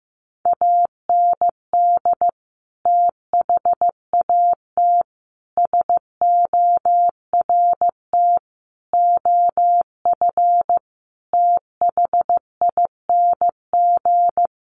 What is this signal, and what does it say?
Morse code, heard as a keyed tone, about 15 wpm, 700 Hz. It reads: AND THAT SORT OF THING
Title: Sets of 5 Words Encore2 15wpm